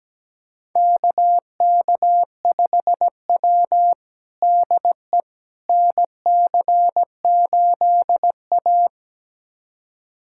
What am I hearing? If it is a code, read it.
Morse code, heard as a keyed tone, about 17 wpm, 700 Hz. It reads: KK5W DE NC8A
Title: USA Callsign DE Callsign - 4 Characters - 17wpm